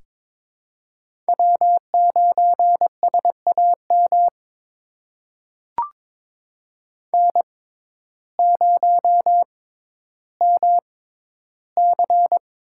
Morse code, heard as a keyed tone, about 22 wpm, spaced as for 12 wpm, 700 Hz. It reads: W9SAM N0MC
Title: USA Call Signs - 5 Characters - 6x Character Spacing - 22wpm